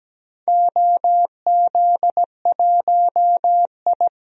Morse code, heard as a keyed tone, about 17 wpm, 700 Hz. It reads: OZ1I
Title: DXCC-100 Call Signs - 4 Characters - 17wpm